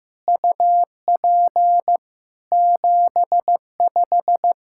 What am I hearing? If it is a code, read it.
Morse code, heard as a keyed tone, about 16 wpm, 700 Hz. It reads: UP 75
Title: Practice QSO - Exchange 2 - Info - 15wpm